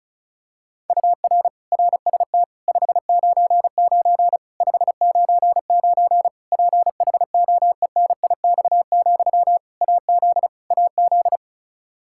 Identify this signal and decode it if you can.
Morse code, heard as a keyed tone, about 34 wpm, 700 Hz. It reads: UR RST 599 599 PHOENIX, AZ AZ
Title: K1USN SST QSO Practice 35wpm